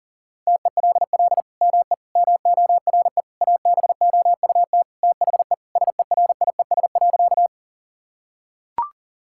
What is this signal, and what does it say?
Morse code, heard as a keyed tone, about 40 wpm, 700 Hz. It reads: TELL ME MORE ABOUT THE SERIES.
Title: Sentences from Top 1000 Words - Review Part 5 - 40wpm